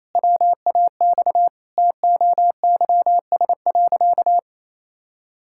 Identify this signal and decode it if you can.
Morse code, heard as a keyed tone, about 28 wpm, 700 Hz. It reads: WAX TOYS.
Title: Pangram Challenge v2 - 28wpm